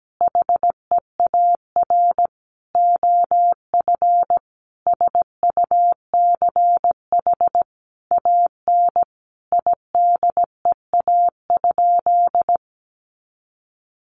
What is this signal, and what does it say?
Morse code, heard as a keyed tone, about 17 wpm, 700 Hz. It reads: HEAR OF SUCH AN IDEA?
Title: Sentences from Top 400 Words 17wpm